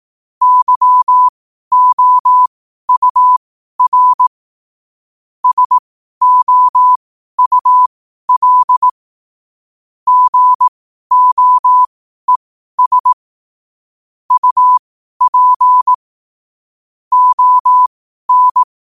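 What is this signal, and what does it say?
Morse code, heard as a keyed tone, about 18 wpm, spaced as for 12 wpm, 1000 Hz. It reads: YOUR SOUL GOES UP ON
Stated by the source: Morse code at 12 wpm.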